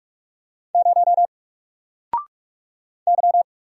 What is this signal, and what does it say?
Morse code, heard as a keyed tone, about 44 wpm, 700 Hz. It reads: 0 Y
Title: Single Character (Letter:Number:Punctuation:Prosign) 45wpm